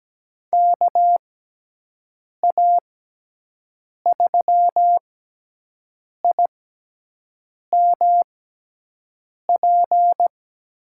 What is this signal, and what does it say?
Morse code, heard as a keyed tone, about 17 wpm, spaced as for 6 wpm, 700 Hz. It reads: KA3IMP
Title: USA Call Signs - 6 Characters - 6x Character Spacing - 17wpm